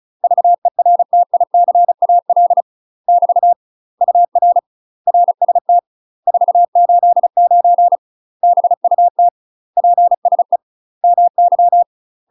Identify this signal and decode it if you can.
Morse code, heard as a keyed tone, about 34 wpm, 700 Hz. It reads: VERTICAL <BT> UR RST 489 BUT PSE MY
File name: UK MASTER QSOs TRAINING/35WPM